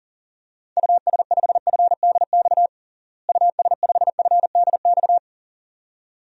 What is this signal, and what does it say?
Morse code, heard as a keyed tone, about 40 wpm, 700 Hz. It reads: US5FDX US5FDX
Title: DXCC-100 Call Signs - 6 Characters - Repeated 2x - 40wpm